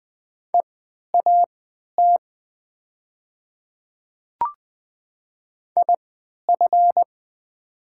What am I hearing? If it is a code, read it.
Morse code, heard as a keyed tone, about 20 wpm, spaced as for 11 wpm, 700 Hz. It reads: EAT IF
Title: 3x Character Spacing - Top 300 Words - 20wpm